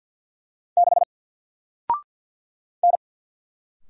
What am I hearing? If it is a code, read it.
Morse code, heard as a keyed tone, about 48 wpm, 700 Hz. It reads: 6 N
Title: Single Character (Letter:Number:Punctuation:Prosign) 50wpm